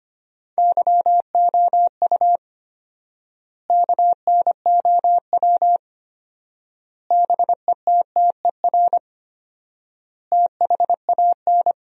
Morse code, heard as a keyed tone, about 25 wpm, 700 Hz. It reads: YOU KNOW BETTER THAN
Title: 4x Word-Spacing - Sentences from Top 400 Words - 25wpm